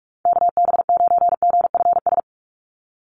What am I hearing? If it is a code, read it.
Morse code, heard as a keyed tone, about 44 wpm, 700 Hz. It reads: KB9GFS
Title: USA Call Signs 45wpm